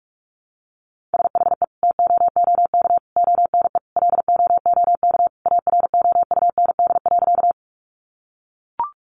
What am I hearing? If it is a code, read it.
Morse code, heard as a keyed tone, about 44 wpm, 700 Hz. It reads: SHE TOOK ONE LOOK AROUND.
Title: Sentences from Top 600 Words 45wpm